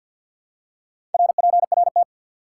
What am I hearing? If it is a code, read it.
Morse code, heard as a keyed tone, about 53 wpm, 700 Hz. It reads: RPRT
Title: Top 90 QSO Elements 50wpm